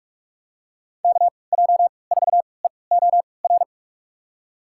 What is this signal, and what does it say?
Morse code, heard as a keyed tone, about 46 wpm, spaced as for 26 wpm, 700 Hz. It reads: KJ4EOR
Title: USA Call Signs - 6 Characters - 3x Character Spacing - 45wpm